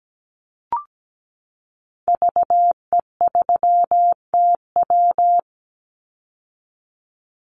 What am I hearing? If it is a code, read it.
Morse code, heard as a keyed tone, about 17 wpm, 700 Hz. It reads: VE3TW
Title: Call Signs - 17wpm